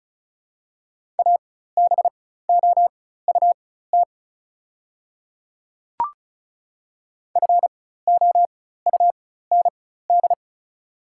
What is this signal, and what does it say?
Morse code, heard as a keyed tone, about 36 wpm, spaced as for 16 wpm, 700 Hz. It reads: ABOUT FOUND
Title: 4x Character Spacing - Top 300 Words - 35wpm